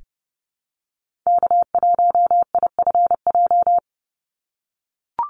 Morse code, heard as a keyed tone, about 30 wpm, 700 Hz. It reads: K1IFJ
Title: Call Signs - 30wpm